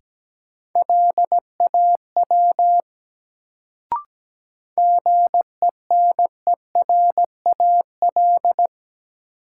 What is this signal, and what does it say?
Morse code, heard as a keyed tone, about 17 wpm, 700 Hz. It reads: LAW GENERAL
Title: Sentences from Top 600 Words - New Word Drill - 17wpm